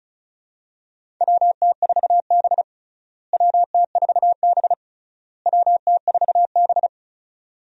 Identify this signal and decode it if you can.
Morse code, heard as a keyed tone, about 36 wpm, 700 Hz. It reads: WT4B WT4B WT4B
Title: USA Call Signs - 4 Characters - Repeated 3x - 35wpm